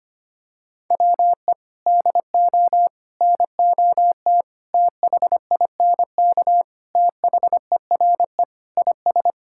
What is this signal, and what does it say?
Morse code, heard as a keyed tone, about 25 wpm, 700 Hz. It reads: WE DO NOT THINK THERE IS
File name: practice / 25wpm / Sentences with 4-Word Phrases - 25wpm